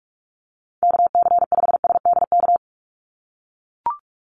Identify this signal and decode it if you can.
Morse code, heard as a keyed tone, about 44 wpm, 700 Hz. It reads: KC5SDK
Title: USA Call Signs 45wpm